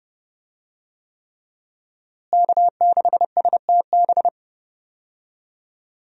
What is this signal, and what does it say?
Morse code, heard as a keyed tone, about 30 wpm, 700 Hz. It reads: K6STB
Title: USA Call Signs - 5 Characters - Repeated 3x - 30wpm